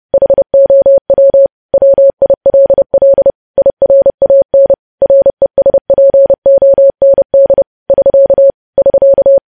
Morse code, heard as a keyed tone, about 30 wpm, 550 Hz. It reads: HOW WILL IRAN RESPOND <SK> <SK>
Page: news headlines in morse code at 30 wpm